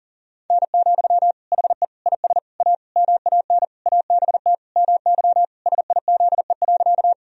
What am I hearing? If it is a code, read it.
Morse code, heard as a keyed tone, about 40 wpm, 700 Hz. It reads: N, HE IS A MAN ABT MY SIZE.
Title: Sentences from Top 400 Words and Abbreviations 40wpm